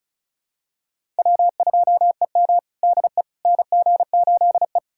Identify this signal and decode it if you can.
Morse code, heard as a keyed tone, about 36 wpm, 700 Hz. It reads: W2EM DE NG8E
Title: USA Callsign DE Callsign - 4 Characters - 35wpm